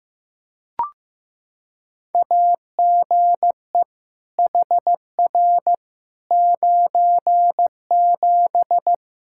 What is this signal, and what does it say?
Morse code, heard as a keyed tone, about 15 wpm, 700 Hz. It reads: AGE HR 97
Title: Practice QSO - Exchange 3 - More Info - 15wpm